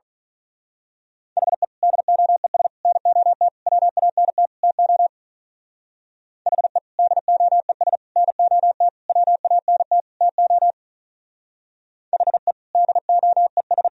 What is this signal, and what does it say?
Morse code, heard as a keyed tone, about 41 wpm, 700 Hz. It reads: HE DOES NOT WANT TO HE DOES NOT WANT TO HE DOES
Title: Speed-Racing - Sets of 5 Words 35wpm